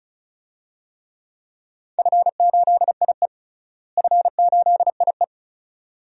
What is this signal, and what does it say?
Morse code, heard as a keyed tone, about 36 wpm, 700 Hz. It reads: F8IE F8IE
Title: DXCC-100 Call Signs - 4 Characters - Repeated 2x - 35wpm